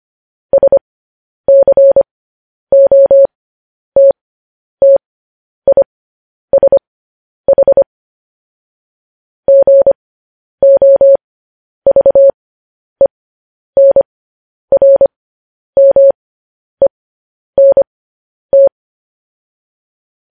Morse code, heard as a keyed tone, about 25 wpm, spaced as for 10 wpm, 550 Hz. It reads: SCOTTISH GOVENRMENT
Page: news-headlines-in-morse-code-at-10-wpm